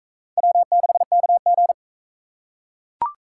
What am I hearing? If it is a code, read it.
Morse code, heard as a keyed tone, about 43 wpm, 700 Hz. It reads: W6KG